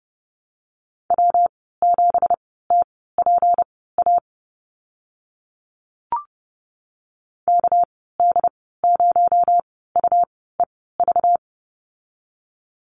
Morse code, heard as a keyed tone, about 30 wpm, spaced as for 17 wpm, 700 Hz. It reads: W7TPA KD0UEV
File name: Lesson 37g - Letter x - Call Signs 30wpm